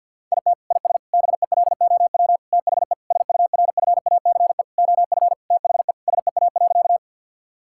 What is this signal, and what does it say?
Morse code, heard as a keyed tone, about 50 wpm, 700 Hz. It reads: IT IS BELOW THE SURFACE OF THE SEA.
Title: Sentences from Top 1000 Words - Review Part 5 - 50wpm